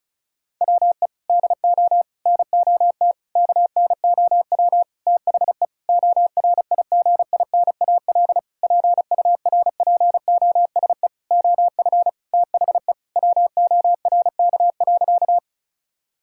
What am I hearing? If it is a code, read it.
Morse code, heard as a keyed tone, about 36 wpm, 700 Hz. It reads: WE DO NOT KNOW THE ORIGINAL PURPOSE OF THE WORK.
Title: Sentences from Top 1000 Words - Review Part 6 - 35wpm